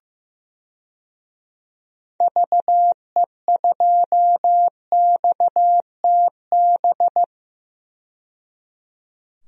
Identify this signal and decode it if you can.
Morse code, heard as a keyed tone, about 15 wpm, 700 Hz. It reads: VE2XTB
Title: DXCC-100 Call Signs - 6 Characters - 15wpm